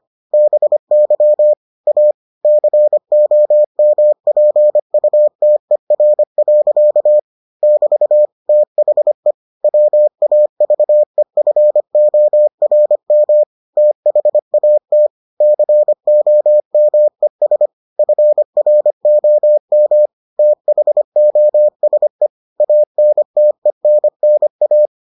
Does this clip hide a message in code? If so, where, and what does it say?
Yes: Morse code from 0.3 to 24.9 s, heard as a keyed tone, about 25 wpm, 600 Hz. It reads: BY A COMPUTER. <BT> THE WAVEFORM THAT COMES FROM THOSE ANTENNA